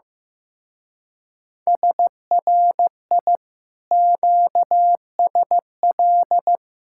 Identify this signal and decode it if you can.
Morse code, heard as a keyed tone, about 16 wpm, 700 Hz. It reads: SRI QSL
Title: Speed-Racing Common QSO Phrases Encore - 15wpm